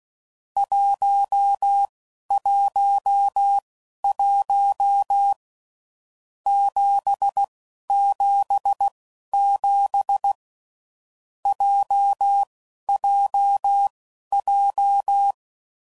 Morse code, heard as a keyed tone, about 16 wpm, spaced as for 12 wpm, 800 Hz. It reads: 111 777 JJJ
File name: Groups / Basic Letter Groups / JZ17